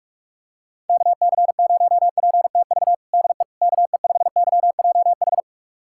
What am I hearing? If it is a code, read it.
Morse code, heard as a keyed tone, about 46 wpm, 700 Hz. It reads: KC0PTV DE KE5YJH
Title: USA Callsign DE Callsign 45wpm